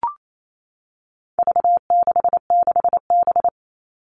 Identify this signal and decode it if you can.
Morse code, heard as a keyed tone, about 28 wpm, 700 Hz. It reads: V66B
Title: All Cut Numbers - 4-Digit 28wpm